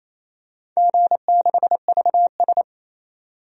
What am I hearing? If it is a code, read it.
Morse code, heard as a keyed tone, about 28 wpm, 700 Hz. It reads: G6VS
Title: DXCC-100 Call Signs - 4 Characters - 28wpm